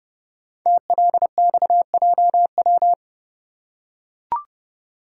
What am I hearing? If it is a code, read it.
Morse code, heard as a keyed tone, about 30 wpm, 690 Hz. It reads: TLXJW
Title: Code Groups - 5 Letters - 30wpm